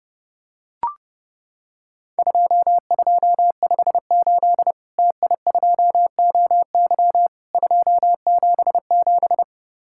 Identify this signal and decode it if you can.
Morse code, heard as a keyed tone, about 30 wpm, 700 Hz. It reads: 2258 TI2OY 277
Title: Worked All Europe DX Contest - QTC Exchange - 30wpm